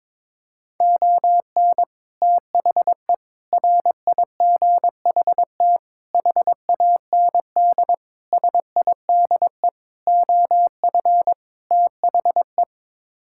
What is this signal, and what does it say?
Morse code, heard as a keyed tone, about 22 wpm, 700 Hz. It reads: ON THE RIGHT HAND SIDE OF THE
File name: practice / 22wpm / Sets of 7 Words - 22wpm